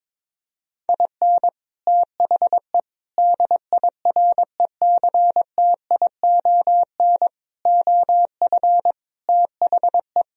Morse code, heard as a keyed tone, about 22 wpm, 700 Hz. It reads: IN THE DIRECTION OF THE
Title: Speed-Racing - Sets of 5 Words 22wpm